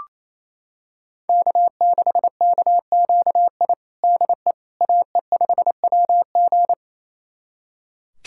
Morse code, heard as a keyed tone, about 28 wpm, 700 Hz. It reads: K6KQI DE AE5WG
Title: USA Callsign DE Callsign - 5 Characters - 28wpm